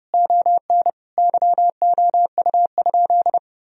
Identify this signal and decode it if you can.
Morse code, heard as a keyed tone, about 30 wpm, 700 Hz. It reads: ON YOU?
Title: Sentences from Top 1000 Words - Review Part 4 - 30wpm